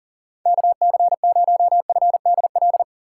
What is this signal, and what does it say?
Morse code, heard as a keyed tone, about 40 wpm, 700 Hz. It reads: KC0FDL
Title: USA Call Signs - 6 Characters - 2x Character Spacing - 40wpm